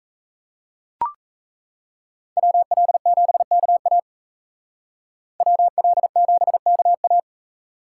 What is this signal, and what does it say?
Morse code, heard as a keyed tone, about 36 wpm, 700 Hz. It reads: WL7KA WL7KA